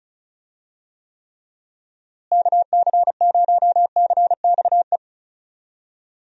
Morse code, heard as a keyed tone, about 36 wpm, 700 Hz. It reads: KC0CXE
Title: USA Call Signs - 6 Characters - Repeated 2x - 35wpm